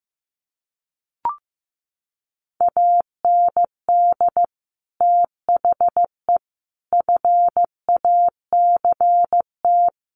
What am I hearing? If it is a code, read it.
Morse code, heard as a keyed tone, about 15 wpm, 700 Hz. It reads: AND THE FACT
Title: Sets of 4 Words 15wpm